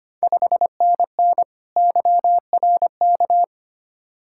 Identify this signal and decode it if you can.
Morse code, heard as a keyed tone, about 25 wpm, 700 Hz. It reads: 5NN YRK
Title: Ontario QSO Party Exchange 25wpm